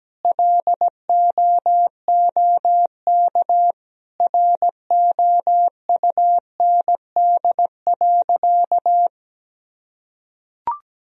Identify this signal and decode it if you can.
Morse code, heard as a keyed tone, about 17 wpm, 700 Hz. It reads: LOOK ROUND.
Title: Sentences from Top 1000 Words - Review Part 1 - 17wpm